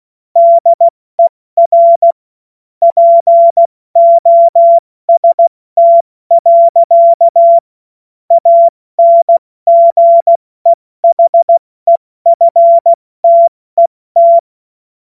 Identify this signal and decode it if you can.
Morse code, heard as a keyed tone, about 16 wpm, spaced as for 14 wpm, 680 Hz. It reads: DER POST. ANGEHEFTET
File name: Morsekurs / Witze / Witz 04. Oktober 2022 70 F80